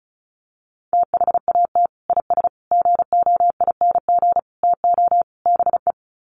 Morse code, heard as a keyed tone, about 34 wpm, 700 Hz. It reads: THAT IS GOING TO BE
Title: Sets of 5 Words Encore2 35wpm